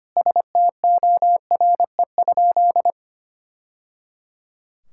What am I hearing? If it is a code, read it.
Morse code, heard as a keyed tone, about 25 wpm, 700 Hz. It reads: STORE?
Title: Sentences from Top 1000 Words - Review Part 6 - 25wpm